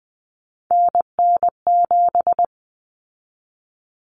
Morse code, heard as a keyed tone, about 20 wpm, 700 Hz. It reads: NN7
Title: Common Cut Numbers - 3-Digit 20wpm